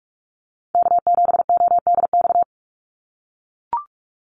Morse code, heard as a keyed tone, about 44 wpm, 700 Hz. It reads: K7ODX
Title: NA SOTA Activator Callsigns 45wpm